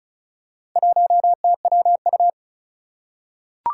Morse code, heard as a keyed tone, about 36 wpm, 700 Hz. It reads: 1TWU
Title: Four Challenging Letter-Number Combinations 35wpm